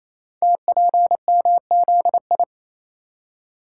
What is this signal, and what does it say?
Morse code, heard as a keyed tone, about 27 wpm, 700 Hz. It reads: TPMZI
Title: Code Groups - 5 Letters - 28wpm